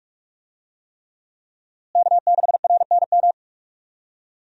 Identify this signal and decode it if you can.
Morse code, heard as a keyed tone, about 46 wpm, 700 Hz. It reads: K6RNM
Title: USA Call Signs - 5 Characters - Repeated 3x - 45wpm